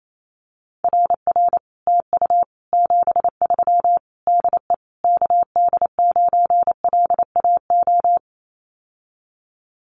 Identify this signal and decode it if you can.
Morse code, heard as a keyed tone, about 28 wpm, 700 Hz. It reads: RR TU 73 DE KD9LAO
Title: Common QSO Phrases Encore 28wpm